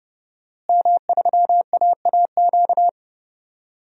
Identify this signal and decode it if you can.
Morse code, heard as a keyed tone, about 32 wpm, 700 Hz. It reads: M3AAQ
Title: DXCC-100 Call Signs - 5 Characters - Repeated 3x - 30wpm